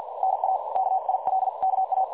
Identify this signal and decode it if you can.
Morse code, heard as a keyed tone, about 50 wpm, spaced as for 43 wpm, 760 Hz, with noise. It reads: NA KILKA
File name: dlazaawansowanych / nagraniazpasma / websdr recording 3565.0kHz